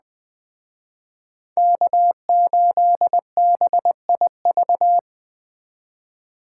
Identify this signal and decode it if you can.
Morse code, heard as a keyed tone, about 21 wpm, 700 Hz. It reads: K8BIV